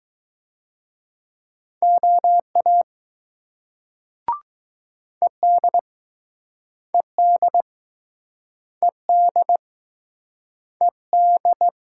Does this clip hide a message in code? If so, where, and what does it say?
Yes: Morse code from 1.8 to 11.7 s, heard as a keyed tone, about 20 wpm, 700 Hz. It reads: OA ED ED ED ED